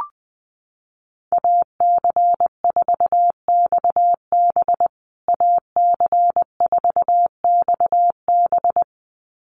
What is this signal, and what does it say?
Morse code, heard as a keyed tone, about 20 wpm, 700 Hz. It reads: AC4XB AC4XB
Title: USA Call Signs Repeated 2x 20wpm